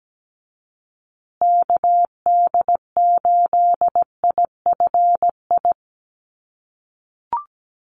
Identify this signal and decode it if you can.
Morse code, heard as a keyed tone, about 17 wpm, 700 Hz. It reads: KD8IFI
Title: USA Call Signs Repeated 2x 17wpm